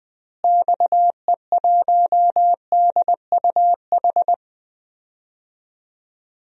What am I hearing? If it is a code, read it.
Morse code, heard as a keyed tone, about 20 wpm, 700 Hz. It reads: XE1DUH
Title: DXCC-100 Call Signs - 6 Characters - Repeated 2x - 20wpm